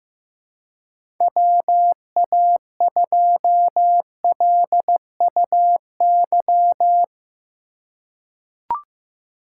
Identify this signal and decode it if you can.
Morse code, heard as a keyed tone, about 15 wpm, 700 Hz. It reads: WA2LUY
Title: USA Call Signs - 6 Characters - 4x Character Spacing - 15wpm